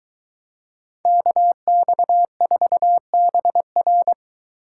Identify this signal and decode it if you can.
Morse code, heard as a keyed tone, about 23 wpm, 700 Hz. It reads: KX4BR